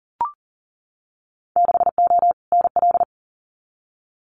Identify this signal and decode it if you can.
Morse code, heard as a keyed tone, about 40 wpm, 700 Hz. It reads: 6O NL
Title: Winter Field Day Exchanges Encore 40wpm